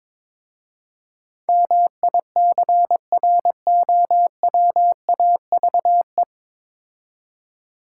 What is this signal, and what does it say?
Morse code, heard as a keyed tone, about 22 wpm, 700 Hz. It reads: MICROWAVE
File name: Easy Words - Speed-Racing with Hints - 22wpm